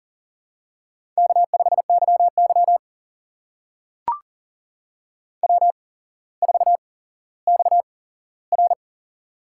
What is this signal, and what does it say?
Morse code, heard as a keyed tone, about 41 wpm, spaced as for 17 wpm, 700 Hz. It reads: K5YY W4XR
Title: USA Call Signs - 4 Characters - 8x Character Spacing - 40wpm